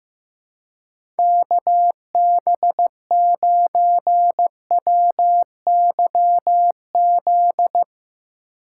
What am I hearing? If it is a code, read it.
Morse code, heard as a keyed tone, about 15 wpm, 700 Hz. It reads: KB9WYZ
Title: USA Call Signs - 4x Character Spacing - 15wpm